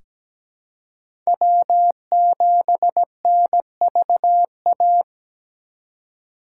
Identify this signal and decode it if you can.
Morse code, heard as a keyed tone, about 17 wpm, 700 Hz. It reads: W7NVA
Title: USA Call Signs - 4x Character Spacing - 17wpm